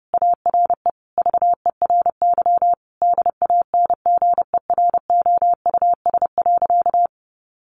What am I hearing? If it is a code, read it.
Morse code, heard as a keyed tone, about 30 wpm, 700 Hz. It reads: ARE VERY DANGEROUS.
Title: Sentences from Top 1200 Words 30wpm